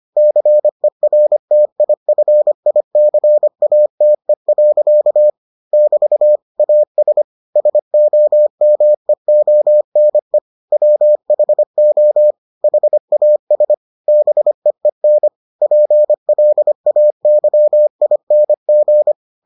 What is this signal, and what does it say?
Morse code, heard as a keyed tone, about 25 wpm, 600 Hz. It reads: CERTIFICATE. <BT> AS SOMEONE WHO HAS BEEN PLAYING